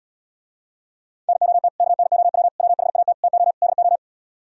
Morse code, heard as a keyed tone, about 77 wpm, 700 Hz. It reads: NOT KNOW WHAT TO DO